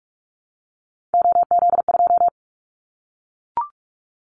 Morse code, heard as a keyed tone, about 44 wpm, 700 Hz. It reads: OZ2
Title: Callsign Prefixes for Top 100 DXCC - 45wpm